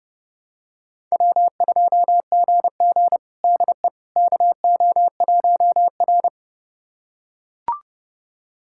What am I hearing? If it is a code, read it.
Morse code, heard as a keyed tone, about 30 wpm, 700 Hz. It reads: W2GG DE KO1R
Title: USA Callsign DE Callsign - 4 Characters - 30wpm